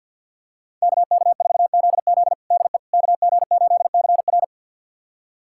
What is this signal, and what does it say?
Morse code, heard as a keyed tone, about 51 wpm, 700 Hz. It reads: KK4ZZ DE KG8CR
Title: USA Callsign DE Callsign - 5 Characters - 50wpm